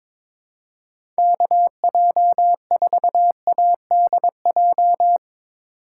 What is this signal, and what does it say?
Morse code, heard as a keyed tone, about 22 wpm, 700 Hz. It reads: KJ4ADJ
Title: USA Call Signs - 6 Characters - 3x Character Spacing - 22wpm